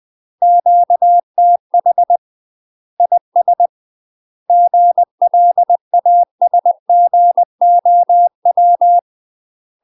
Morse code, heard as a keyed tone, about 20 wpm, 700 Hz. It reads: QTH IS GLASGOW
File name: UK MASTER QSOs TRAINING/20WPM